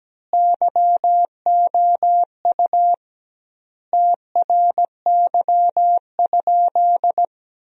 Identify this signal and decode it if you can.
Morse code, heard as a keyed tone, about 17 wpm, 700 Hz. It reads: YOU TRY?
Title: 2x Word-Spacing - Sentences from Top 200 Words - 17wpm